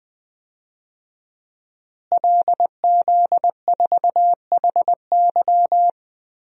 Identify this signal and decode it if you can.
Morse code, heard as a keyed tone, about 20 wpm, 700 Hz. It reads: LZ4HY
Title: DXCC-100 Call Signs - 5 Characters - Repeated 3x - 20wpm